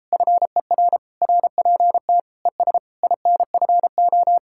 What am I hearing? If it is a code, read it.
Morse code, heard as a keyed tone, about 33 wpm, 700 Hz. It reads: FER RPT ES INFO